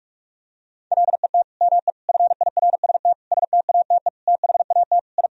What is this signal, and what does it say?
Morse code, heard as a keyed tone, about 41 wpm, 700 Hz. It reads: LET ME FIRST STATE THAT I
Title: Sentences from Top 1000 Words - Review Part 2 - 45wpm